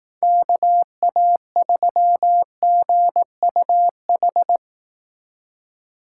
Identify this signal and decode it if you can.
Morse code, heard as a keyed tone, about 19 wpm, 700 Hz. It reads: KA3GUH
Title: Speed Racing - USA 2x3 Call Signs 15wpm